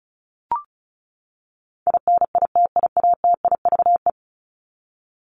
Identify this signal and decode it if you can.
Morse code, heard as a keyed tone, about 34 wpm, 700 Hz. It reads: INITIATIVE
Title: Top 1751-2000 Words 35wpm